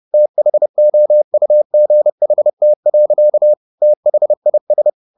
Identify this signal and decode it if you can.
Morse code, heard as a keyed tone, about 30 wpm, 600 Hz. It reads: THOUGHT. THIS